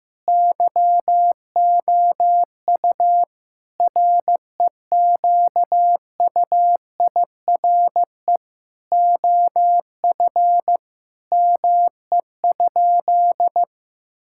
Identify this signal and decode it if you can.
Morse code, heard as a keyed tone, about 16 wpm, 700 Hz. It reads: YOU REQUIRE OF ME?
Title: Sentences from Top 1000 Words - Review Part 6 - 15wpm